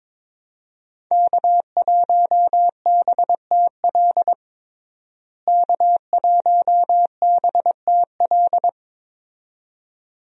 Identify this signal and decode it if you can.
Morse code, heard as a keyed tone, about 22 wpm, 700 Hz. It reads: K1BTL K1BTL
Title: USA Call Signs - 5 Characters - Repeated 2x - 22wpm